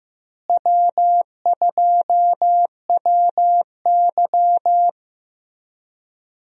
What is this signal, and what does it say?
Morse code, heard as a keyed tone, about 15 wpm, 700 Hz. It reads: W2WY
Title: USA Call Signs - 4 Characters - Repeated 3x - 15wpm